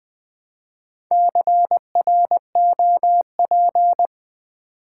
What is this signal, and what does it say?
Morse code, heard as a keyed tone, about 20 wpm, 700 Hz. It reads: CROP
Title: Build A Word - 2 Letters At A Time - 4-Letter Words - 20wpm